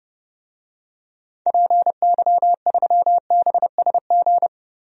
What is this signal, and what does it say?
Morse code, heard as a keyed tone, about 30 wpm, 700 Hz. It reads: PY3BSG
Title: DXCC-100 Call Signs - 6 Characters - 30wpm